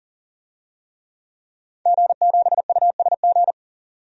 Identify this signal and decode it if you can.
Morse code, heard as a keyed tone, about 40 wpm, 700 Hz. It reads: G7USG
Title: DXCC-100 Call Signs - 5 Characters - 40wpm